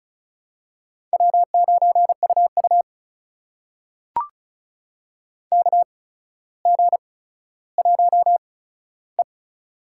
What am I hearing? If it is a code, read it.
Morse code, heard as a keyed tone, about 36 wpm, spaced as for 15 wpm, 700 Hz. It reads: W9UU KG1E
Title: USA Call Signs - 4 Characters - 8x Character Spacing - 35wpm